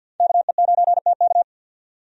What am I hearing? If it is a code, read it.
Morse code, heard as a keyed tone, about 47 wpm, 700 Hz. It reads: KE9TK
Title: USA Call Signs - 5 Characters - 50wpm